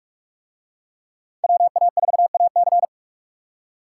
Morse code, heard as a keyed tone, about 40 wpm, 700 Hz. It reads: WA4AC